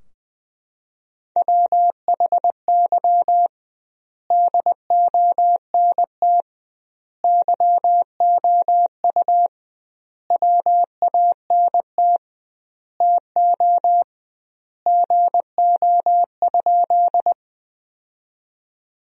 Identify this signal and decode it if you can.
Morse code, heard as a keyed tone, about 20 wpm, 700 Hz. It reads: WHY DONT YOU WANT TO GO?
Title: 2x Word-Spacing - Sentences from Top 300 Words - 20wpm